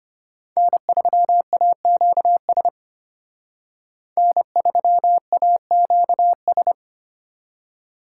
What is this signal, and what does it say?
Morse code, heard as a keyed tone, about 30 wpm, 700 Hz. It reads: N3AQS N3AQS
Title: Speed Racing - USA 1x3 Call Signs 25wpm